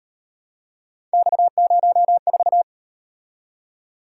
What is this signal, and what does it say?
Morse code, heard as a keyed tone, about 38 wpm, 700 Hz. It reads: X04